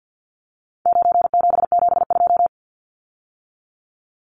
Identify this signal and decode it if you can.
Morse code, heard as a keyed tone, about 50 wpm, 700 Hz. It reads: 9772
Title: Numbers - 4-Digit 50wpm